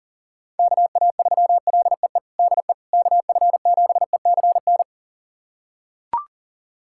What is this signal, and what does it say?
Morse code, heard as a keyed tone, about 37 wpm, 700 Hz. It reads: KA3LEE DE KF7ECN
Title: USA Callsign DE Callsign - 6 Characters - 40wpm